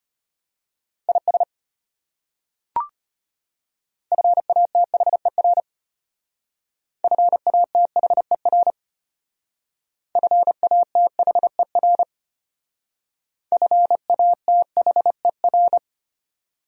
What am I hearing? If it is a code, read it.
Morse code, heard as a keyed tone, about 31 wpm, 700 Hz. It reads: IS FATHER FATHER FATHER FATHER
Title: Speed-Racing Top 500 Words 25wpm